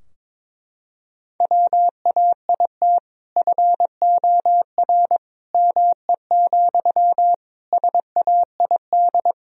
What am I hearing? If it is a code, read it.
Morse code, heard as a keyed tone, about 22 wpm, 700 Hz. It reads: WAIT FOR ME, SAID
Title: Sentences from Top 1000 Words - Review Part 4 - 22wpm